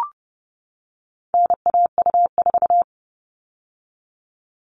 Morse code, heard as a keyed tone, about 30 wpm, 700 Hz. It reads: NAU4
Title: All Cut Numbers - 4-Digit 30wpm